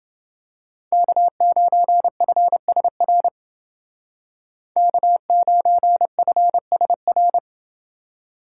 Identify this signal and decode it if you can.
Morse code, heard as a keyed tone, about 29 wpm, 700 Hz. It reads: K9FSR K9FSR